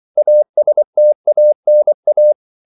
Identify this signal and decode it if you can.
Morse code, heard as a keyed tone, about 24 wpm, 600 Hz. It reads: ASTANA
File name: CW / Practice / NationalCapitals24